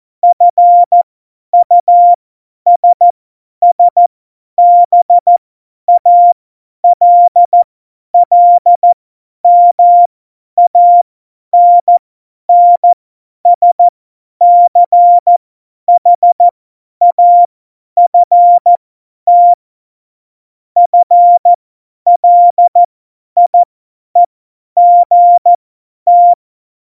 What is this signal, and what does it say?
Morse code, heard as a keyed tone, about 14 wpm, spaced as for 10 wpm, 700 Hz. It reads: FUSSBALLMANNSCHAFT FLIEGT
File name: Morsekurs / Witze / Witz 05. Januar 2021, 50 F70